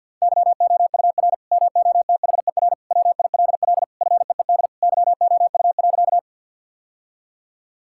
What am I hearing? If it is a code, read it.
Morse code, heard as a keyed tone, about 50 wpm, 700 Hz. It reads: YOUR MOTHER WILL FEED YOU.
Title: Sentences from Top 1600 Words 50wpm